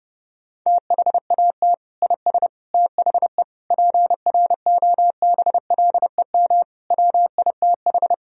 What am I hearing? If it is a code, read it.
Morse code, heard as a keyed tone, about 30 wpm, 700 Hz. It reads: THAT IS THE PROBLEM WITH
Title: Sentences from Top 1000 Words - Review Part 2 - 30wpm